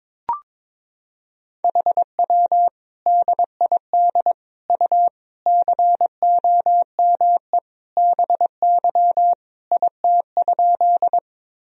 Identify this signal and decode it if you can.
Morse code, heard as a keyed tone, about 22 wpm, 700 Hz. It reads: HW DID U COME BY IT?
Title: Sentences from Top 100 Words and Abbreviations 22wpm